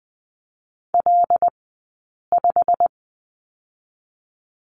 Morse code, heard as a keyed tone, about 20 wpm, spaced as for 8 wpm, 690 Hz. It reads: L5
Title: 197 Lesson 32d - Letter k - Two Characters 20wpm